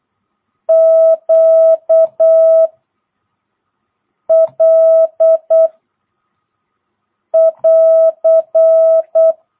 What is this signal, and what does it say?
Morse code, heard as a keyed tone, about 8 wpm, 660 Hz. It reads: Q L <AR>
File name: jh8cbh / 2022cw